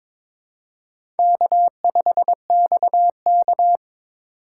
Morse code, heard as a keyed tone, about 22 wpm, 700 Hz. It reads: K5XK
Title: K1USN SST QSO Practice - 22wpm